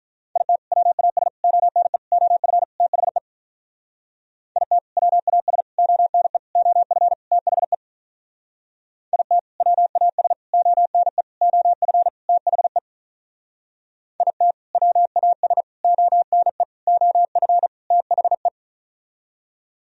Morse code, heard as a keyed tone, about 46 wpm, 700 Hz. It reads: IT WAS ONE OF THE IT WAS ONE OF THE IT WAS ONE OF THE IT WAS ONE OF THE